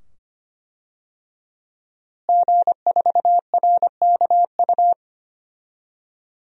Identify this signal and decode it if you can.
Morse code, heard as a keyed tone, about 25 wpm, 700 Hz. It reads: G4RKU
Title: DXCC-100 Call Signs - 5 Characters - Repeated 2x - 25wpm